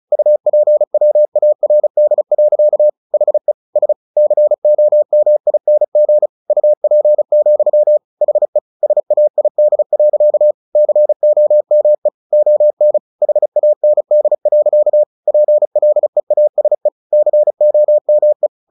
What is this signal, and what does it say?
Morse code, heard as a keyed tone, about 34 wpm, 600 Hz. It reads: UPWARD. HE S COMING UP, HE SAID. COME ON HAND. PLEASE COME
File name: the old man and the sea / the old man and the sea 1 35wpm